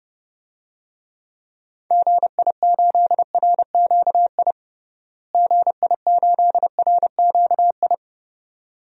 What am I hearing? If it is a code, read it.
Morse code, heard as a keyed tone, about 30 wpm, 700 Hz. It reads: GI8RQI GI8RQI
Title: DXCC-100 Call Signs - 6 Characters - Repeated 2x - 30wpm